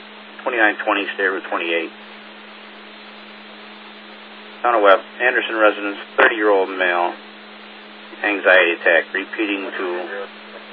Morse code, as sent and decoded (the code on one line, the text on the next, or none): none